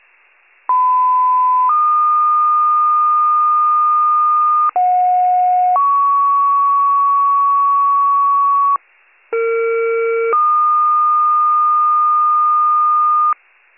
Two-tone page, heard as a one-second tone, then a three-second tone, about 1010 then 1220 Hz, about 730 then 1120 Hz, about 460 then 1220 Hz. It is heard from 0.7 to 13.3 s.